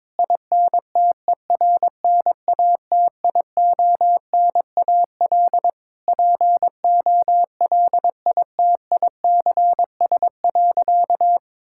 Morse code, heard as a keyed tone, about 22 wpm, 700 Hz. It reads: INTERNATIONAL POLITICS.
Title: Sentences from Top 1000 Words - Review Part 6 - 22wpm